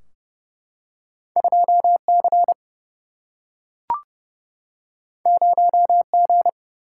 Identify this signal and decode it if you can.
Morse code, heard as a keyed tone, about 30 wpm, 700 Hz. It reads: 2 C 0 G
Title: Single-Letter Single-Number 30wpm